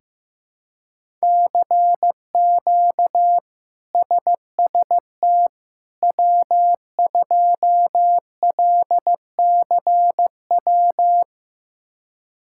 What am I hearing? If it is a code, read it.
Morse code, heard as a keyed tone, about 15 wpm, 700 Hz. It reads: CQ SST W2LCW
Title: K1USN SST QSO Practice - 15wpm